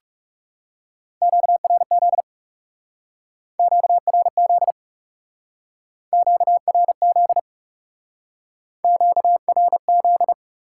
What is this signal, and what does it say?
Morse code, heard as a keyed tone, about 38 wpm, 700 Hz. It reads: QRZ QRZ QRZ QRZ
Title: Speed-Racing Top 90 QSO Elements 30wpm